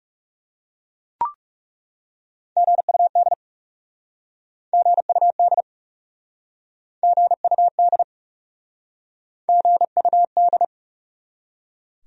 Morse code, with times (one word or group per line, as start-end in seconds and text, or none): 2.6-3.3 GUD
4.7-5.6 GUD
7.0-8.0 GUD
9.5-10.6 GUD